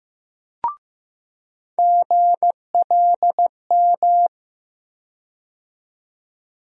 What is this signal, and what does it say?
Morse code, heard as a keyed tone, about 15 wpm, 700 Hz. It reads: GLM
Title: USA Call Sign Postfixes - 3 Letters 15wpm